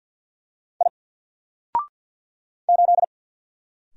Morse code, heard as a keyed tone, about 48 wpm, 700 Hz. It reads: I 8
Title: Single Character (Letter:Number:Punctuation:Prosign) 50wpm